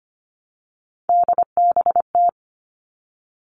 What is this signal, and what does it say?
Morse code, heard as a keyed tone, about 25 wpm, 700 Hz. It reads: DBT
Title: All Cut Numbers - 3-Digit 25wpm